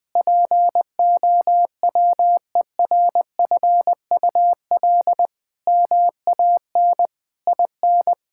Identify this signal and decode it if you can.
Morse code, heard as a keyed tone, about 20 wpm, 700 Hz. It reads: POWERFUL MAN IN
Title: Sentences with 7-Word Phrases - Spoken Beginning - 20wpm